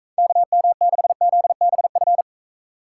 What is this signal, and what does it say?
Morse code, heard as a keyed tone, about 42 wpm, 700 Hz. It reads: KM6ZBF